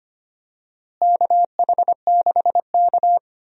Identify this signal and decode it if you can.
Morse code, heard as a keyed tone, about 25 wpm, 700 Hz. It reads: KH6K
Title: DXCC-100 Call Signs - 4 Characters - Repeated 3x - 25wpm